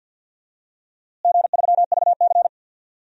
Morse code, heard as a keyed tone, about 51 wpm, 700 Hz. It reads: G3VC
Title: DXCC-100 Call Signs - 4 Characters - Repeated 3x - 50wpm